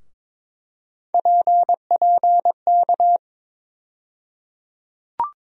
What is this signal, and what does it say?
Morse code, heard as a keyed tone, about 22 wpm, 700 Hz. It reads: PPK
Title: USA Call Sign Postfixes - 3 Letters 22wpm